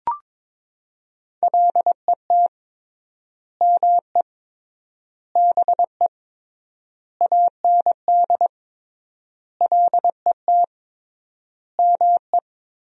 Morse code, heard as a keyed tone, about 22 wpm, 700 Hz. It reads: LET ME BE AND LET ME
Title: 3x Word-Spacing - Sentences from Top 300 Words - 22wpm